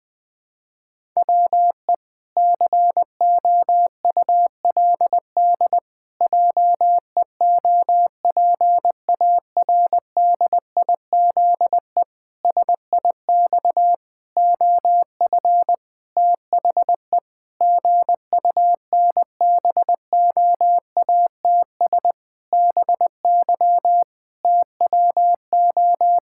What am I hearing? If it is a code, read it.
Morse code, heard as a keyed tone, about 19 wpm, 700 Hz. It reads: WE COULD JEOPARDIZE SIX OF THE GUNBOATS BY TWO
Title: Pangram Challenge v2 - 20wpm